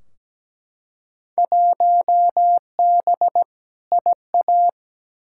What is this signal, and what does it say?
Morse code, heard as a keyed tone, about 17 wpm, 700 Hz. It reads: 1B IA
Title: Field Day Exchanges Encore - 17wpm